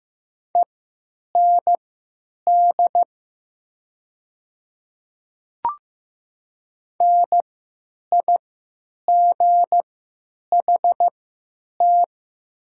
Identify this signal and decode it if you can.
Morse code, heard as a keyed tone, about 15 wpm, spaced as for 9 wpm, 700 Hz. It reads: END NIGHT
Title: 3x Character Spacing - Top 300 Words - 15wpm